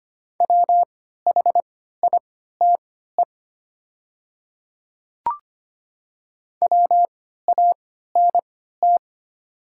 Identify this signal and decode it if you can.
Morse code, heard as a keyed tone, about 25 wpm, spaced as for 14 wpm, 700 Hz. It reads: WHITE WANT
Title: 3x Character Spacing - Top 300 Words - 25wpm